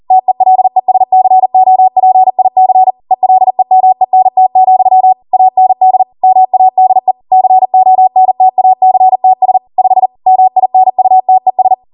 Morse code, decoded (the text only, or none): TELESCOPIC ELEMENT, AND MADE CONTACTS 5 MINUTES